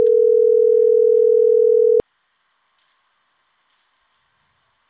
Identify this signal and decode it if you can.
Ringback tone (North America), 440 + 480 Hz.